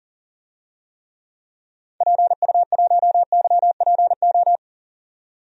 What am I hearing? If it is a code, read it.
Morse code, heard as a keyed tone, about 37 wpm, 700 Hz. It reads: PU1YPO
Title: DXCC-100 Call Signs - 6 Characters - 40wpm